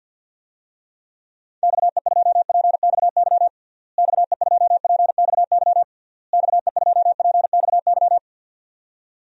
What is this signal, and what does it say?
Morse code, heard as a keyed tone, about 52 wpm, 700 Hz. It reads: XE2PXY XE2PXY XE2PXY
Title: DXCC-100 Call Signs - 6 Characters - Repeated 3x - 50wpm